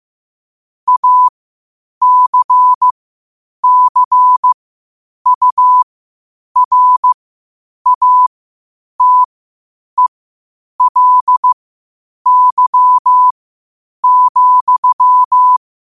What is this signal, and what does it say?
Morse code, heard as a keyed tone, about 15 wpm, spaced as for 8 wpm, 1000 Hz. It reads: ACCURATELY,
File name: qotdfiles / 8wpm